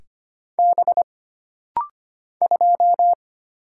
Morse code, heard as a keyed tone, about 25 wpm, 700 Hz. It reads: B 2
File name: Single Character (Letter:Number:Punctuation:Prosign) - Rapid-Fire - 25wpm